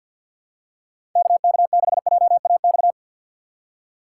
Morse code, heard as a keyed tone, about 53 wpm, 700 Hz. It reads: KK6JAX
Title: USA Call Signs - 6 Characters - 3x Character Spacing - 50wpm